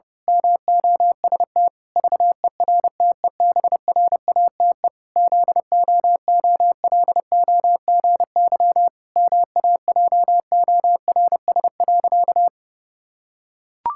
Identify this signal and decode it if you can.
Morse code, heard as a keyed tone, about 28 wpm, 700 Hz. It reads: MOST VERTEBRATE ZOOLOGY MAJORS.
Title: Pangram Challenge v2 - 30wpm